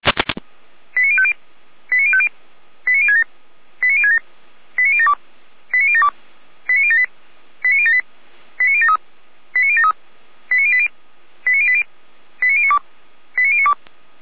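Five-tone selective call (ZVEI), heard as quick runs of five tones, tones of about 70 ms. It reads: 89050 89050 89067 89067 89072 89072 89078 89078 89073 89073 89080 89080 89092 89092
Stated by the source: ZVEI five-tone selective call received on an SDR.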